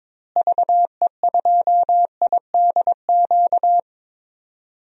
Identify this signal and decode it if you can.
Morse code, heard as a keyed tone, about 22 wpm, 700 Hz. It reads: VE2IDQ
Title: DXCC-100 Call Signs - 6 Characters - Repeated 2x - 22wpm